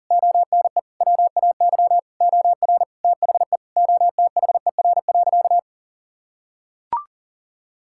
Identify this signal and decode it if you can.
Morse code, heard as a keyed tone, about 37 wpm, 700 Hz. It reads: ONE WAY OR THE OTHER.
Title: Sentences from Top 1000 Words - Review Part 6 - 40wpm